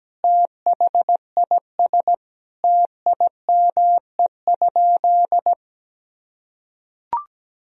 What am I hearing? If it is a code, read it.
Morse code, heard as a keyed tone, about 17 wpm, 700 Hz. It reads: THIS TIME?
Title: Sentences from Top 1000 Words - Review Part 3 - 17wpm